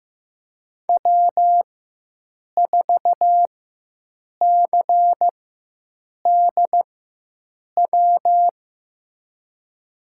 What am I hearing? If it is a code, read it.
Morse code, heard as a keyed tone, about 16 wpm, spaced as for 7 wpm, 700 Hz. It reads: W4CDW
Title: USA Call Signs - 5 Characters - 4x Character Spacing - 15wpm